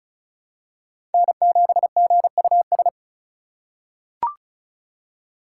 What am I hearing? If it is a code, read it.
Morse code, heard as a keyed tone, about 34 wpm, 700 Hz. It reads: N7GUS
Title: USA Call Signs - 2x Character Spacing - 35wpm